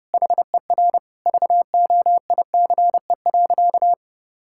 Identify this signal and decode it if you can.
Morse code, heard as a keyed tone, about 32 wpm, 700 Hz. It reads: HER VOICE.
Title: Sentences from Top 1600 Words 30wpm